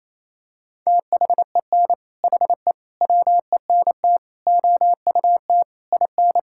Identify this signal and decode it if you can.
Morse code, heard as a keyed tone, about 28 wpm, 700 Hz. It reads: THEN HE WENT OUT IN
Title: Sentences from Top 1000 Words - Review Part 5 - 28wpm